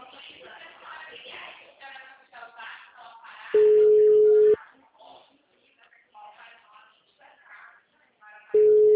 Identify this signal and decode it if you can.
Ringback tone (Europe, 425 Hz), with speech.